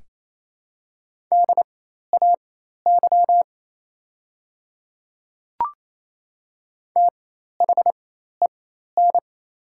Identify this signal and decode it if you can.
Morse code, heard as a keyed tone, about 28 wpm, spaced as for 13 wpm, 700 Hz. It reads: DAY THEN
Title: 4x Character Spacing - Top 100 Words - 28wpm